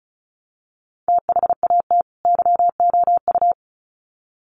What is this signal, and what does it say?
Morse code, heard as a keyed tone, about 34 wpm, 700 Hz. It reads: THAT YOU
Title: Sets of 2 Words 35wpm